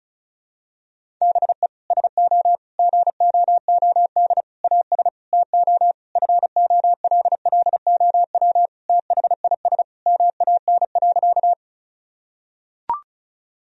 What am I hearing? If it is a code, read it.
Morse code, heard as a keyed tone, about 36 wpm, 700 Hz. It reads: BE SO GOOD AS TO FOLLOW THIS MAN.
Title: Sentences from Top 1000 Words - Review Part 1 - 35wpm